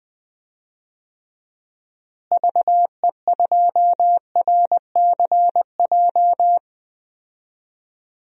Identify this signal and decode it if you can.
Morse code, heard as a keyed tone, about 21 wpm, 700 Hz. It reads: VE2RCJ
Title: DXCC-100 Call Signs - 6 Characters - Repeated 2x - 20wpm